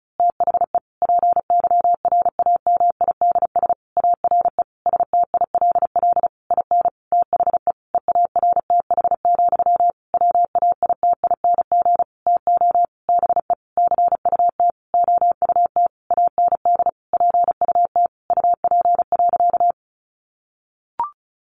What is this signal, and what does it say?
Morse code, heard as a keyed tone, about 34 wpm, 700 Hz. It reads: THE PYRAMIDS ARE STILL IN THE EARTH, WAITING TO BE CUT OUT AND PUT UP.
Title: A Sound Of Thunder 35wpm